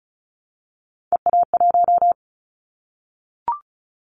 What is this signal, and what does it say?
Morse code, heard as a keyed tone, about 36 wpm, 700 Hz. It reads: EA1
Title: Callsign Prefixes for Top 25 DXCC - 35wpm